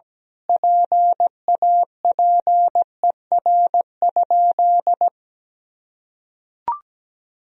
Morse code, heard as a keyed tone, about 17 wpm, 700 Hz. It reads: PAPER?
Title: Sentences from Top 1000 Words - Review Part 2 - 17wpm